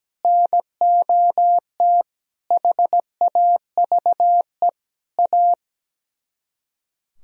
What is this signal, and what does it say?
Morse code, heard as a keyed tone, about 17 wpm, 700 Hz. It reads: NOT HAVE A
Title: Speed-Racing - Sets of 5 Words 17wpm